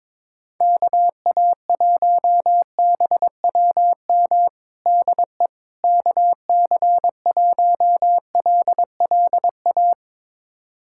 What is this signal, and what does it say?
Morse code, heard as a keyed tone, about 22 wpm, 700 Hz. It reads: KA1BWM DE KC1LLA
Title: USA Callsign DE Callsign 22wpm